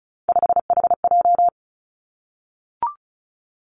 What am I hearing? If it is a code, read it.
Morse code, heard as a keyed tone, about 36 wpm, 700 Hz. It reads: 5HJ
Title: Three Challenging Letter-Number Combinations 35wpm